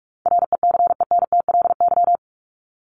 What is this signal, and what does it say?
Morse code, heard as a keyed tone, about 44 wpm, 700 Hz. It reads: RECENTLY
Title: LY Suffix - Top 100 Words with Base Words 45wpm